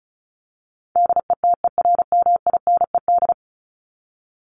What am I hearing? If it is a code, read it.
Morse code, heard as a keyed tone, about 34 wpm, 700 Hz. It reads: DETERMINED
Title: ED Suffix - Top 100 Words with Base Words 35wpm